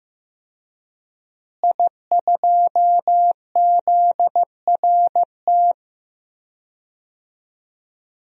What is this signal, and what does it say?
Morse code, heard as a keyed tone, about 15 wpm, 700 Hz. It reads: I2ZRT
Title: DXCC-100 Call Signs - 5 Characters - Repeated 3x - 15wpm